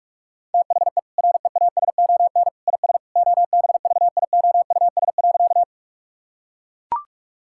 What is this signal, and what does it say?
Morse code, heard as a keyed tone, about 41 wpm, 700 Hz. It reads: THE REASON IS OBVIOUS.
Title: Sentences from Top 1600 Words 45wpm